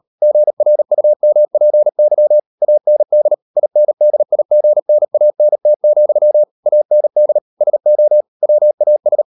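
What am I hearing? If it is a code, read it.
Morse code, heard as a keyed tone, about 38 wpm, 600 Hz. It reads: GRUMPY AND INDIGNANT, AND SO WAS
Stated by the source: Morse code practice at 38 WPM, sending Ken Follett's novel Never in English, from CW med Gnister.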